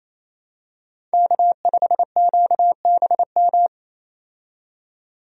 Morse code, heard as a keyed tone, about 28 wpm, 700 Hz. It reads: K5QBM
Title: USA Call Signs - 5 Characters - 28wpm